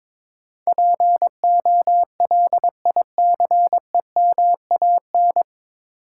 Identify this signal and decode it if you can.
Morse code, heard as a keyed tone, about 22 wpm, 700 Hz. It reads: POLICEMAN
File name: 9 Letter Words - First 3 Letters Given - 22wpm